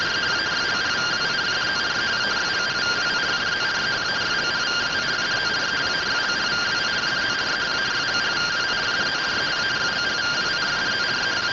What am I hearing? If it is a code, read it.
Two-tone teleprinter signal, about 170 Hz shift.
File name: MR CORONAL fr1 -131.gen-28 rtty rx